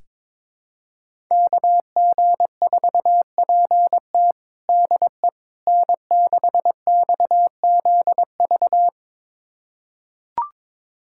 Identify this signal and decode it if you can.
Morse code, heard as a keyed tone, about 22 wpm, 700 Hz. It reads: KG4PT DE N6XZV
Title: USA Callsign DE Callsign - 5 Characters - 22wpm